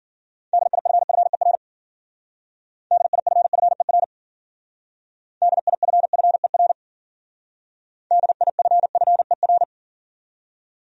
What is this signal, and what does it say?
Morse code, heard as a keyed tone, about 50 wpm, 700 Hz. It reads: DIFFER DIFFER DIFFER DIFFER
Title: Speed-Racing Top 200 Words 40wpm